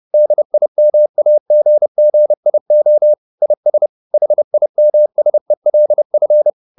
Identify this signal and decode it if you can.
Morse code, heard as a keyed tone, about 30 wpm, 600 Hz. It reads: DIMAGGIO IS HIMSELF
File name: the old man and the sea / the old man and the sea 1 30wpm